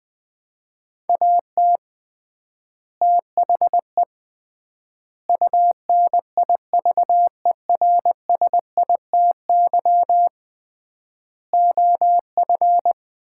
Morse code, heard as a keyed tone, about 20 wpm, 700 Hz. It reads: AT THE UNIVERSITY OF
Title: 3x Word-Spacing - Sets of 4 Words - 20wpm